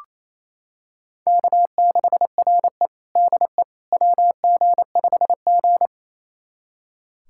Morse code, heard as a keyed tone, about 28 wpm, 700 Hz. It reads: K6RE DE WG5G
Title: USA Callsign DE Callsign - 4 Characters - 28wpm